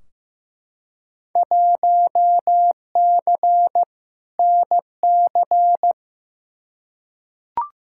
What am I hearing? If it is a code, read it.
Morse code, heard as a keyed tone, about 15 wpm, 700 Hz. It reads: 1C NC
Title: Field Day Exchanges - 15wpm